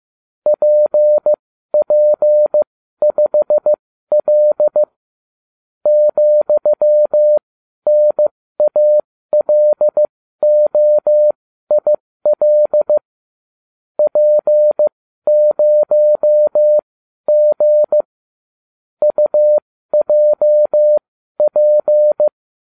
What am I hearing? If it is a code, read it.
Morse code, heard as a keyed tone, about 15 wpm, spaced as for 12 wpm, 600 Hz. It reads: PP5L ,NALOIL P0G UJP